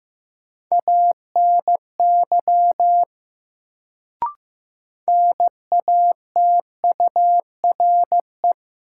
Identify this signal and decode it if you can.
Morse code, heard as a keyed tone, about 15 wpm, 700 Hz. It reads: ANY NATURE
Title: Sets of 3 Words - New Word Drill - 15wpm